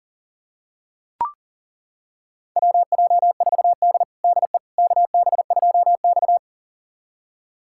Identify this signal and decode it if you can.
Morse code, heard as a keyed tone, about 40 wpm, 700 Hz. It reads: WJ4D DE KB2X
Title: USA Callsign DE Callsign - 4 Characters - 40wpm